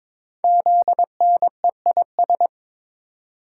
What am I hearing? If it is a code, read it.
Morse code, heard as a keyed tone, about 22 wpm, 700 Hz. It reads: ZNEIS
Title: Code Groups - 5 Letters - 22wpm